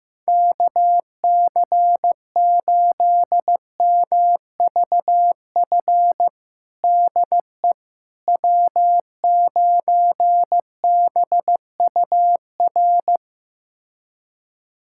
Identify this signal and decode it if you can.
Morse code, heard as a keyed tone, about 15 wpm, 700 Hz. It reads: KC8MVF DE W9BUR
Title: USA Callsign DE Callsign 15wpm